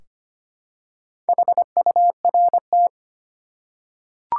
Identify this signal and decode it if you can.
Morse code, heard as a keyed tone, about 25 wpm, 700 Hz. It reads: HURT
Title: Four Letter Words - 25wpm